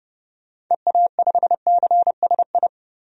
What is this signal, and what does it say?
Morse code, heard as a keyed tone, about 30 wpm, 700 Hz. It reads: EA5CSI
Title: DXCC-100 Call Signs - 6 Characters - 30wpm